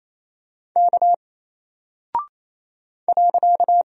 Morse code, heard as a keyed tone, about 28 wpm, 700 Hz. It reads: K .
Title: Single Character (Letter:Number:Punctuation:Prosign) Alt 28wpm